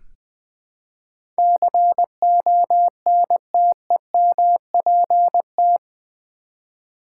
Morse code, heard as a keyed tone, about 20 wpm, 700 Hz. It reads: CONTEMPT
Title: Build A Word - 2 Letters At A Time - 8-Letter Words - 20wpm